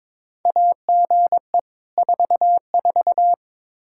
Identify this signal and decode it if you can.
Morse code, heard as a keyed tone, about 22 wpm, 700 Hz. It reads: AGE 44
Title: Practice QSO - Exchange 3 - More Info - 22wpm